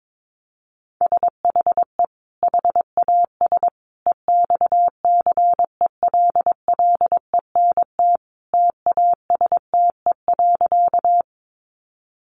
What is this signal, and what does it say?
Morse code, heard as a keyed tone, about 22 wpm, 700 Hz. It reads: SHE HAS EXCELLENT TASTE.
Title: Sentences from Top 1100 Words 22wpm